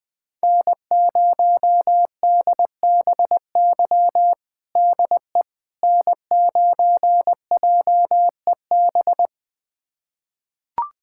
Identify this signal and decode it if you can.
Morse code, heard as a keyed tone, about 20 wpm, 700 Hz. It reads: N0DBY DE N9JEB
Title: USA Callsign DE Callsign 20wpm